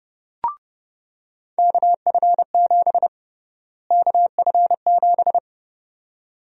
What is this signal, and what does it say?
Morse code, heard as a keyed tone, about 29 wpm, 700 Hz. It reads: KF7 KF7
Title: USA Call Sign Prefixes - Repeated 2x - 30wpm